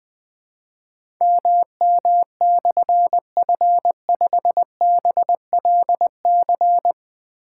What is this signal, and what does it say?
Morse code, heard as a keyed tone, about 20 wpm, 700 Hz. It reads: MM/F5BLC
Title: Difficult Call Signs 20wpm